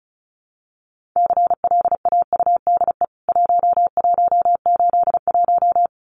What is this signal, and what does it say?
Morse code, heard as a keyed tone, about 34 wpm, 700 Hz. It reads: CLAUDE 1181
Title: CWT Exchange 35wpm